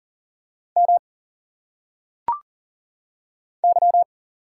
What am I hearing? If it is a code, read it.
Morse code, heard as a keyed tone, about 40 wpm, 700 Hz. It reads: M Y
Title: Single Letters - Phonetic Pronunciation 40wpm